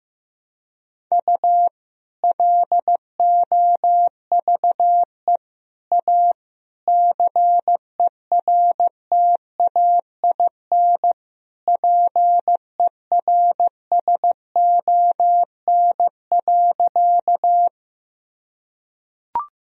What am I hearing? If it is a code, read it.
Morse code, heard as a keyed tone, about 15 wpm, 700 Hz. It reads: U LOVE A CERTAIN PERSON.
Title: Sentences from Top 500 Words and Abbreviations 15wpm